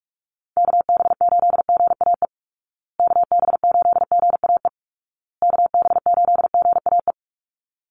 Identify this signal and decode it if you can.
Morse code, heard as a keyed tone, about 44 wpm, 700 Hz. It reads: KB8GAE KB8GAE KB8GAE
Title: Call Signs Repeated 2x - 45wpm